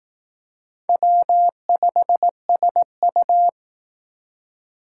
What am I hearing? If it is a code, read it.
Morse code, heard as a keyed tone, about 18 wpm, 700 Hz. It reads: W5SU
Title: Speed Racing - USA 1x2 Call Signs 15wpm